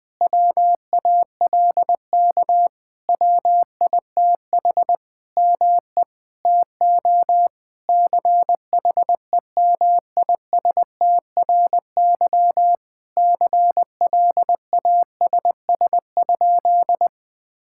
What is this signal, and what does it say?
Morse code, heard as a keyed tone, about 20 wpm, 700 Hz. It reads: WALK WITH ME TO CHEMISTRY CLASS?
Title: Sentences from Top 1000 Words - Review Part 6 - 20wpm